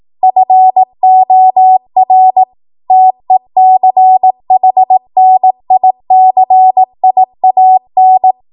Morse code, heard as a keyed tone, about 18 wpm, 750 Hz. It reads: FOR TECHNICIAN
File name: Morse / 18WPM